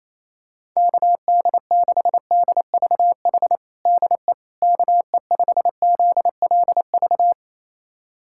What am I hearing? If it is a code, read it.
Morse code, heard as a keyed tone, about 28 wpm, 700 Hz. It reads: KD6DVH DE KE5ZLV
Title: USA Callsign DE Callsign 28wpm